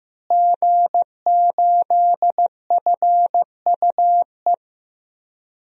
Morse code, heard as a keyed tone, about 15 wpm, 700 Hz. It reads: G8FUE
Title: DXCC-100 Call Signs - 5 Characters - Repeated 3x - 15wpm